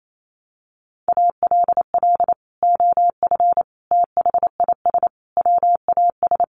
Morse code, heard as a keyed tone, about 28 wpm, 700 Hz. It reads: ALL OF THIS WAS
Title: Sentences from Top 500 Words 28wpm